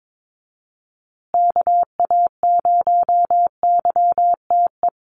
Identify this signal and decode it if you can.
Morse code, heard as a keyed tone, about 22 wpm, 700 Hz. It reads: KA0YTE
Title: USA Call Signs Repeated 2x 22wpm